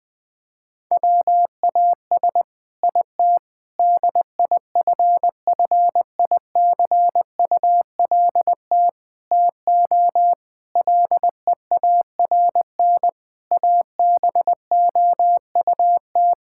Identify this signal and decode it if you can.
Morse code, heard as a keyed tone, about 20 wpm, 700 Hz. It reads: WAS IT DIFFICULT TO LEARN ABOUT
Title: Sentences from Top 1000 Words - Review Part 6 - 20wpm